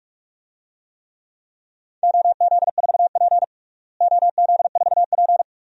Morse code, heard as a keyed tone, about 46 wpm, 700 Hz. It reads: OZ4P OZ4P
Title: DXCC-100 Call Signs - 4 Characters - Repeated 2x - 45wpm